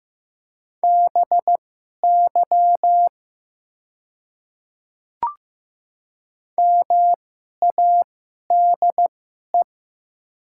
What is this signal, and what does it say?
Morse code, heard as a keyed tone, about 15 wpm, spaced as for 11 wpm, 700 Hz. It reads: BY MADE
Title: 2x Character Spacing - Top 200 Words - 15wpm